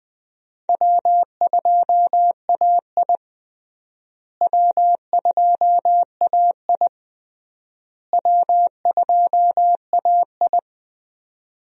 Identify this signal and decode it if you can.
Morse code, heard as a keyed tone, about 20 wpm, 700 Hz. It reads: W2AI W2AI W2AI
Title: USA Call Signs - 4 Characters - Repeated 3x - 20wpm